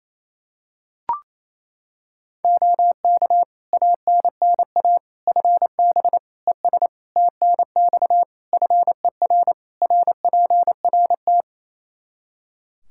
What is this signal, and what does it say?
Morse code, heard as a keyed tone, about 28 wpm, 700 Hz. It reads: OK ANNA FB ES TNX FER RPRT
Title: Practice QSO - Exchange 2 - Info - 28wpm